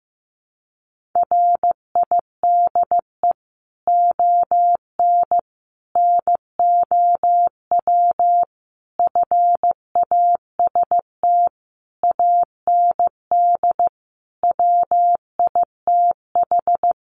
Morse code, heard as a keyed tone, about 15 wpm, 700 Hz. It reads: RIDE ON NOW FAST AND WITH
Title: Sentences from Top 1500 Words 15wpm